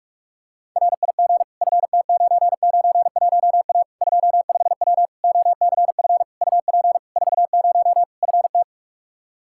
Morse code, heard as a keyed tone, about 42 wpm, 700 Hz. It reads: RIG FT991A 25W OCF UP 40 FT
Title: Practice QSO - Exchange 2 - Info - 45wpm